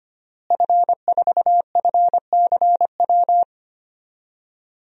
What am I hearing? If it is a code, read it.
Morse code, heard as a keyed tone, about 25 wpm, 700 Hz. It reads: F4FCW
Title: DXCC-100 Call Signs - 5 Characters - Repeated 3x - 25wpm